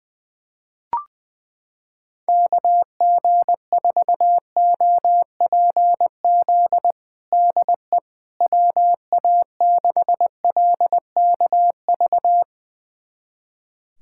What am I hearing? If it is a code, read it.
Morse code, heard as a keyed tone, about 20 wpm, 700 Hz. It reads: KG4OPZ DE WA6LKV
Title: USA Callsign DE Callsign - 6 Characters - 20wpm